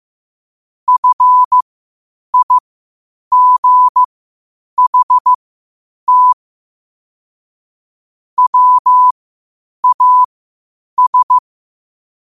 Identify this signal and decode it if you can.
Morse code, heard as a keyed tone, about 15 wpm, spaced as for 8 wpm, 1000 Hz. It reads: FIGHT WAS
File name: qotdfiles / 8wpm